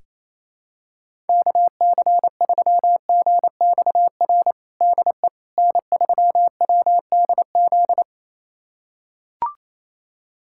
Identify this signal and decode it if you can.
Morse code, heard as a keyed tone, about 28 wpm, 700 Hz. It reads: KC3GXR DE N3WDZ
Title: USA Callsign DE Callsign 28wpm